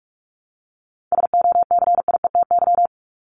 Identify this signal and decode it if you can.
Morse code, heard as a keyed tone, about 44 wpm, 700 Hz. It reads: SOCIETY
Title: Top 501-750 Words 45wpm